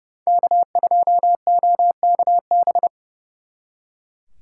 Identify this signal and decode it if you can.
Morse code, heard as a keyed tone, about 36 wpm, 700 Hz. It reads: K2OKB
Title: Speed Racing - USA 1x3 Call Signs 30wpm